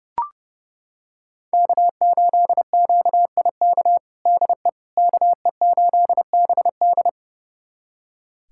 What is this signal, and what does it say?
Morse code, heard as a keyed tone, about 32 wpm, 700 Hz. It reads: K8QIK DE KE8BD
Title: USA Callsign DE Callsign - 5 Characters - 30wpm